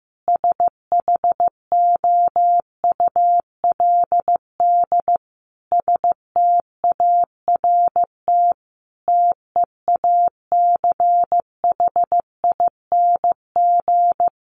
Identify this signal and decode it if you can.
Morse code, heard as a keyed tone, about 15 wpm, 700 Hz. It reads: SHOULD START TEACHING
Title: Sentences from Top 1100 Words 15wpm